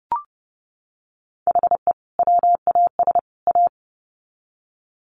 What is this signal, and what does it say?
Morse code, heard as a keyed tone, about 30 wpm, 700 Hz. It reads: HE WAS A
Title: Sets of 3 Words 30wpm